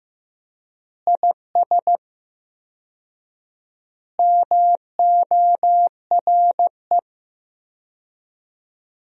Morse code, heard as a keyed tone, about 15 wpm, 700 Hz. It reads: IS MORE
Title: 4x Word-Spacing - Sentences from Top 100 Words - 15wpm